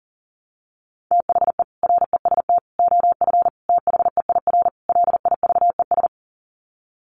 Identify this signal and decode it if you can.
Morse code, heard as a keyed tone, about 40 wpm, 700 Hz. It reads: THE REST OF THEIR LIVES
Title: Sets of 5 Words Encore 40wpm